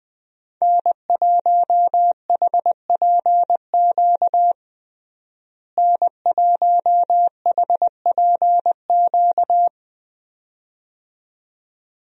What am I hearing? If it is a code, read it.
Morse code, heard as a keyed tone, about 21 wpm, 700 Hz. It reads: N1HPQ N1HPQ
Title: USA Call Signs - 5 Characters - Repeated 2x - 20wpm